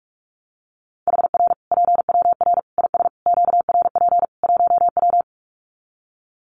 Morse code, heard as a keyed tone, about 44 wpm, 700 Hz. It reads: HR PWR IS QRP 1W
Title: Common QSO Phrases 45wpm